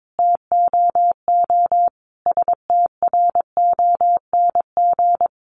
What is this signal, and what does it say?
Morse code, heard as a keyed tone, about 22 wpm, 700 Hz. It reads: TOO STRONG
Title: Sentences from Top 600 Words 22wpm